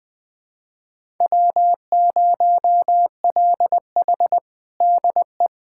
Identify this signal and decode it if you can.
Morse code, heard as a keyed tone, about 21 wpm, 700 Hz. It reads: W0LH DE
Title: USA Callsign DE Callsign 20wpm